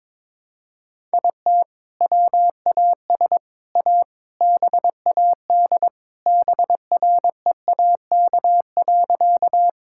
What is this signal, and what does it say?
Morse code, heard as a keyed tone, about 22 wpm, 700 Hz. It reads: IT WAS A BAD BREAK.
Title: Sentences from Top 1000 Words - Review Part 6 - 22wpm